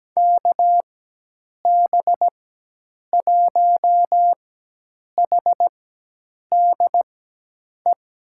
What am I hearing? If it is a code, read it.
Morse code, heard as a keyed tone, about 17 wpm, spaced as for 8 wpm, 700 Hz. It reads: KB1HDE
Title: USA Call Signs - 6 Characters - 4x Character Spacing - 17wpm